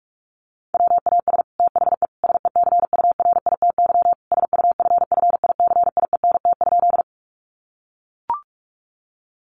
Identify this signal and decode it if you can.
Morse code, heard as a keyed tone, about 44 wpm, 700 Hz. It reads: WAS THE SECURITY SUFFICIENT?
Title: Sentences from Top 1400 Words 45wpm